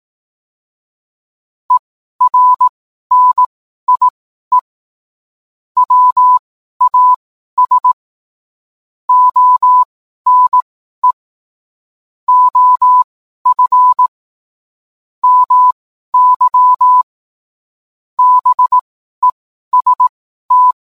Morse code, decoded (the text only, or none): ERNIE WAS ONE OF MY BEST